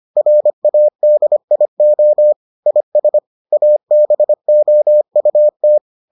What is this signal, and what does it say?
Morse code, heard as a keyed tone, about 25 wpm, 600 Hz. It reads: RADIO IS ABOUT